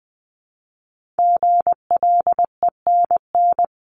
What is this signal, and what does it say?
Morse code, heard as a keyed tone, about 20 wpm, 700 Hz. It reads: GLENN
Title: CWT Exchange 20wpm